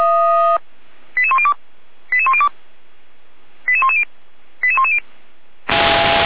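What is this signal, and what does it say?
Five-tone selective call (ZVEI), heard as quick runs of five tones, tones of about 70 ms. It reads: 88102 88102 88110 88110
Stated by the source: Five-tone ZVEI selective call 88192 received on an SDR, from the feuerwehr-orlamuende.org archive.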